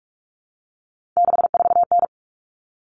Morse code, heard as a keyed tone, about 44 wpm, 700 Hz. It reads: 64N
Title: Common Cut Numbers - 3-Digit 45wpm